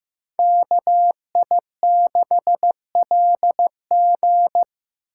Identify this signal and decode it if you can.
Morse code, heard as a keyed tone, about 17 wpm, 700 Hz. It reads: KI6LG
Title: Speed Racing - USA 2x2 Call Signs 15wpm